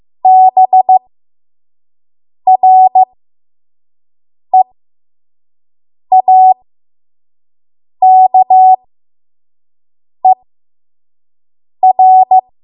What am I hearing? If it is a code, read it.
Morse code, heard as a keyed tone, about 15 wpm, spaced as for 5 wpm, 750 Hz. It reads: BREAKER
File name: Morse / 05WPM